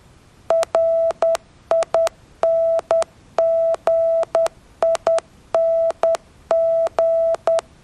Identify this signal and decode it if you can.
Morse code, heard as a keyed tone, about 10 wpm, 660 Hz. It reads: RINGING